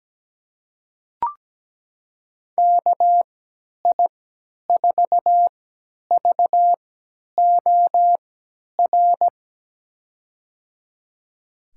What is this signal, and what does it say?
Morse code, heard as a keyed tone, about 17 wpm, spaced as for 10 wpm, 700 Hz. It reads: KI4VOR
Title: USA Call Signs - 3x Character Spacing - 17wpm